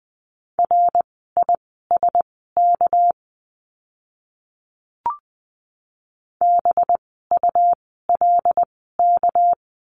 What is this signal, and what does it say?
Morse code, heard as a keyed tone, about 20 wpm, spaced as for 14 wpm, 700 Hz. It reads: RISK BULK
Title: 199 Lesson 32f - Letter k - Words 20wpm